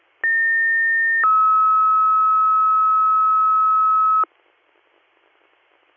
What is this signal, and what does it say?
Two-tone page, heard as a one-second tone, then a three-second tone, about 1800 then 1290 Hz. It is heard from 0.2 to 4.2 s.